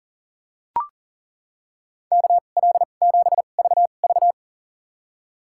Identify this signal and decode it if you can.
Morse code, heard as a keyed tone, about 37 wpm, spaced as for 28 wpm, 700 Hz. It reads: KL7VV
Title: USA Call Signs - 5 Characters - 2x Character Spacing - 40wpm